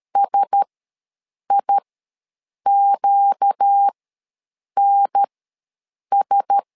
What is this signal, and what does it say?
Morse code, heard as a keyed tone, about 13 wpm, spaced as for 7 wpm, 800 Hz. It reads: SIQNS